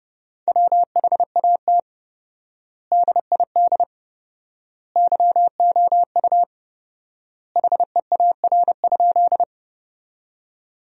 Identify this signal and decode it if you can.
Morse code, heard as a keyed tone, about 30 wpm, 700 Hz. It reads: WHAT DID YOU HEAR?
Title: 4x Word-Spacing - Sentences from Top 300 Words - 30wpm